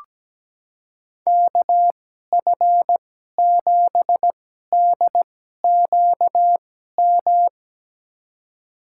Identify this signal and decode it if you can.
Morse code, heard as a keyed tone, about 17 wpm, spaced as for 12 wpm, 700 Hz. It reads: KF7DQM
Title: USA Call Signs - 2x Character Spacing - 17wpm